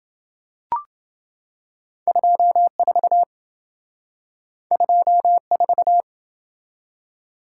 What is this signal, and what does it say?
Morse code, heard as a keyed tone, about 27 wpm, 700 Hz. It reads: 24 24